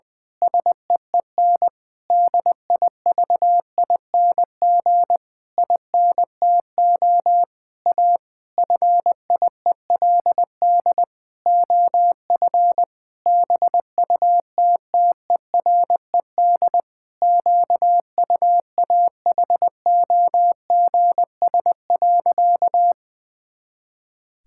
Morse code, heard as a keyed tone, about 20 wpm, 700 Hz. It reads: SEEN DIVING INTO A FIELD OF BUTTERED QUAHOGS.
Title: Pangram Challenge v2 - 20wpm